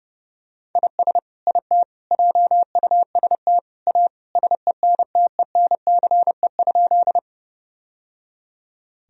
Morse code, heard as a keyed tone, about 32 wpm, 700 Hz. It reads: IS IT JUST A SENTENCE?
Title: Sentences from Top 1000 Words - Review Part 1 - 30wpm